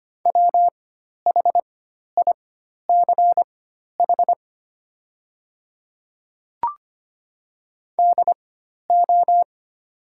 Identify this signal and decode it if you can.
Morse code, heard as a keyed tone, about 25 wpm, spaced as for 12 wpm, 700 Hz. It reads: WHICH DO
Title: 4x Character Spacing - Top 100 Words - 25wpm